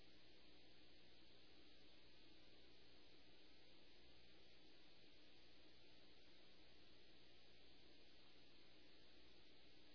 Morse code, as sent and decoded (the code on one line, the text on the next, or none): none